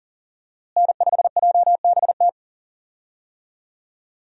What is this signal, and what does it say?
Morse code, heard as a keyed tone, about 40 wpm, 700 Hz. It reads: N5JBT
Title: USA Call Signs - 5 Characters - Repeated 3x - 40wpm